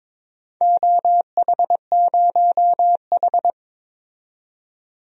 Morse code, heard as a keyed tone, about 22 wpm, 700 Hz. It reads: OH0H
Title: DXCC-100 Call Signs - 4 Characters - 22wpm